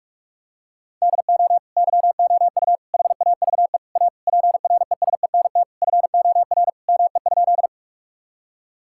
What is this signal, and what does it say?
Morse code, heard as a keyed tone, about 46 wpm, 700 Hz. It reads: DO YOU HAVE A PRESENT FOR ME?
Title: Sentences from Top 1000 Words - Review Part 5 - 45wpm